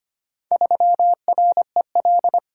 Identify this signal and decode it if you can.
Morse code, heard as a keyed tone, about 25 wpm, 690 Hz. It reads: 3REL
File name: practice / 25wpm / Four Letter-Number Combinations - 25wpm